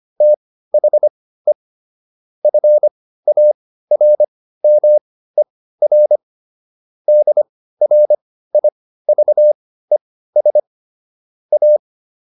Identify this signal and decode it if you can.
Morse code, heard as a keyed tone, about 25 wpm, spaced as for 15 wpm, 600 Hz. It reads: THE FARMER DRIVES A